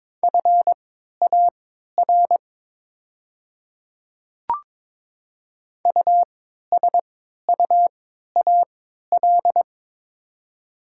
Morse code, heard as a keyed tone, about 22 wpm, spaced as for 12 wpm, 690 Hz. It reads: FAR USUAL
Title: 3x Character Spacing - Top 500 Words - 22wpm